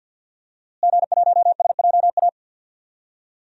Morse code, heard as a keyed tone, about 52 wpm, 700 Hz. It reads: G1SJA
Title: DXCC-100 Call Signs - 5 Characters - Repeated 2x - 50wpm